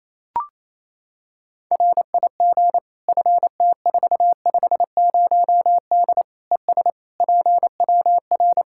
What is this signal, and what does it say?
Morse code, heard as a keyed tone, about 28 wpm, 700 Hz. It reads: RIG FT450D ES PWR
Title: Practice QSO - Exchange 2 - Info - 28wpm